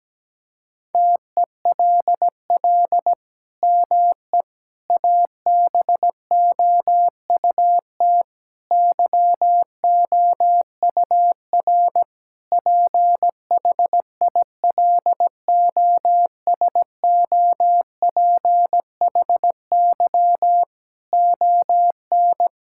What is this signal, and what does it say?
Morse code, heard as a keyed tone, about 17 wpm, 700 Hz. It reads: TELL ME ABOUT YOUR PHILOSOPHY ON
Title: Sentences from Top 1000 Words - Review Part 6 - 17wpm